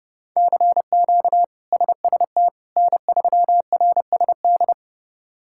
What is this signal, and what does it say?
Morse code, heard as a keyed tone, about 30 wpm, 700 Hz. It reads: CQ SST N3RSD
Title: K1USN SST QSO Practice - 30wpm